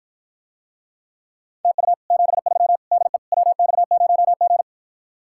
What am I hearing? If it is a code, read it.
Morse code, heard as a keyed tone, about 53 wpm, 700 Hz. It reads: TU 73 DE WX0G